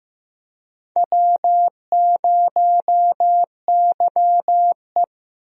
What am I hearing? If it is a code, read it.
Morse code, heard as a keyed tone, about 15 wpm, 700 Hz. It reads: W0YE
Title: USA Call Signs - 4 Characters - Repeated 3x - 15wpm